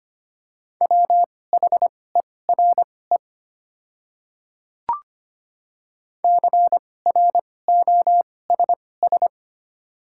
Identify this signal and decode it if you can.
Morse code, heard as a keyed tone, about 25 wpm, spaced as for 18 wpm, 700 Hz. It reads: WHERE CROSS
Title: 2x Character Spacing - Top 300 Words - 25wpm